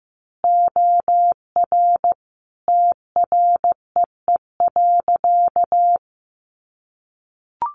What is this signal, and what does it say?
Morse code, heard as a keyed tone, about 15 wpm, 700 Hz. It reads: OR TREE.
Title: A Sound Of Thunder 15wpm